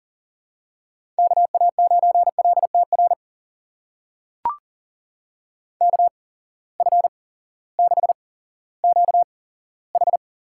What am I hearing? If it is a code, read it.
Morse code, heard as a keyed tone, about 41 wpm, spaced as for 18 wpm, 700 Hz. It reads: KA9LTR KF6QH
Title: USA Call Signs - 6 Characters - 8x Character Spacing - 40wpm